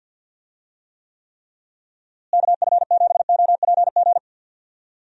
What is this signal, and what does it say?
Morse code, heard as a keyed tone, about 45 wpm, 700 Hz. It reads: KF7OPG
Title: USA Call Signs - 6 Characters - Repeated 3x - 50wpm